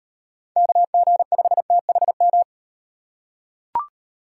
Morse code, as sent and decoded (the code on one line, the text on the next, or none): -.- --. ..... - .... --
KG5THM